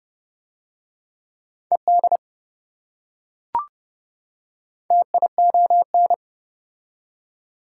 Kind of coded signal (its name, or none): Morse code